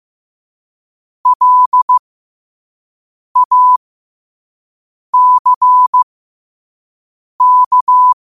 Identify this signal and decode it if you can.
Morse code, heard as a keyed tone, about 15 wpm, spaced as for 5 wpm, 1000 Hz. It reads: LACK
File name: qotdfiles / 5wpm